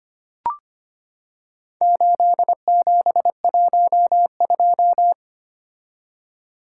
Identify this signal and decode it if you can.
Morse code, heard as a keyed tone, about 25 wpm, 700 Hz. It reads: 8712
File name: practice / 25wpm / Four Letter-Number Combinations - 25wpm